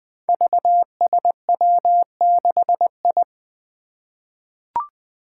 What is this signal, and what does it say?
Morse code, heard as a keyed tone, about 20 wpm, 700 Hz. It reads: VSW6I
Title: Five Challenging Letter-Number Combinations 20wpm